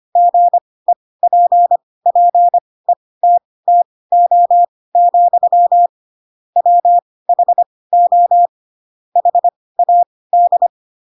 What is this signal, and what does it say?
Morse code, heard as a keyed tone, about 25 wpm, spaced as for 18 wpm, 700 Hz. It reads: GEPPETTO, WHO HAD
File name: Pinocchio / Pinocchio ch7 12 25 ik1hge